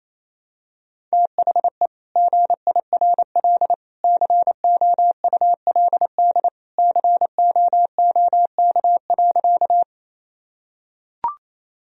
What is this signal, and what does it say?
Morse code, heard as a keyed tone, about 27 wpm, 700 Hz. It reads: THE GIRL COULD COOK.
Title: Sentences from Top 1000 Words - Review Part 4 - 28wpm